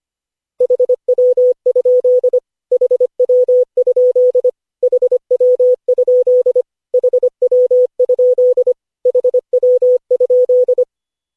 Morse code, heard as a keyed tone, about 25 wpm, 500 Hz. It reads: HW? HW? HW? HW? HW?
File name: cw / Session5 cw training